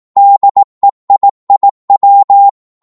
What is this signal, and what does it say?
Morse code, heard as a keyed tone, about 18 wpm, 800 Hz. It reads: DEIIW
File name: Morse / ltr-18wpm